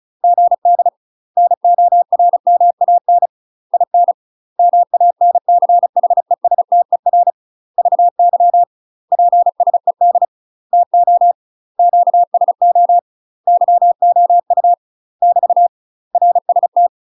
Morse code, heard as a keyed tone, about 34 wpm, 700 Hz. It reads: GD NORMAN IN MANCHESTER VY PSED TO QSO YOU <BT> RST
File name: UK MASTER QSOs TRAINING / 35WPM / UK MASTER QSOs 0003